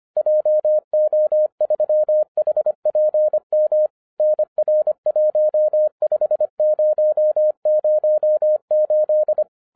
Morse code, heard as a keyed tone, about 25 wpm, 610 Hz. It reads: JO3HPM NR15008
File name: 2011-10-mp3 / n14-MORSE EXPRESS 2011 CHRISTMAS KEY 25wpm